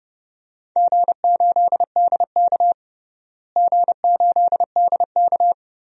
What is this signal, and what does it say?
Morse code, heard as a keyed tone, about 30 wpm, 700 Hz. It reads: G8DK G8DK
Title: DXCC-100 Call Signs - 4 Characters - Repeated 3x - 30wpm